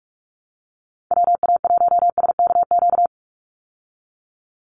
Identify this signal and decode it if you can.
Morse code, heard as a keyed tone, about 46 wpm, 700 Hz. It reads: WA1SKQ
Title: Call Signs - 45wpm